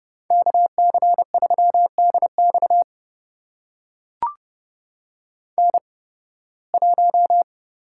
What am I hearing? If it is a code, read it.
Morse code, heard as a keyed tone, about 30 wpm, spaced as for 20 wpm, 700 Hz. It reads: KC3DX N1
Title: USA Call Signs - 5 Characters - 8x Character Spacing - 30wpm